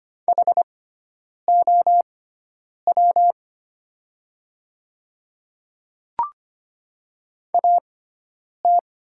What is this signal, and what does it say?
Morse code, heard as a keyed tone, about 25 wpm, spaced as for 9 wpm, 700 Hz. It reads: HOW AT
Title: 6x Character Spacing - Top 100 Words - 25wpm